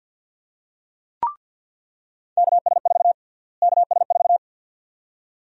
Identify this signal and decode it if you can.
Morse code, heard as a keyed tone, about 47 wpm, 700 Hz. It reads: KS4 KS4
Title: USA Call Sign Prefixes - Repeated 2x - 50wpm